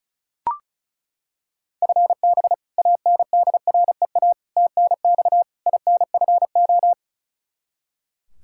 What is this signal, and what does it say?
Morse code, heard as a keyed tone, about 36 wpm, 700 Hz. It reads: FB ANDREA TNX INFO
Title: Practice QSO - Exchange 3 - More Info - 35wpm